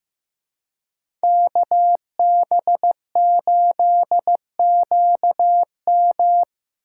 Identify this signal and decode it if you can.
Morse code, heard as a keyed tone, about 18 wpm, 700 Hz. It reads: KB8QM
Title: Speed Racing - USA 2x2 Call Signs 15wpm